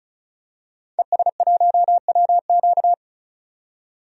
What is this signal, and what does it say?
Morse code, heard as a keyed tone, about 34 wpm, 700 Hz. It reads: ES1WQ
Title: DXCC-100 Call Signs - 5 Characters - 35wpm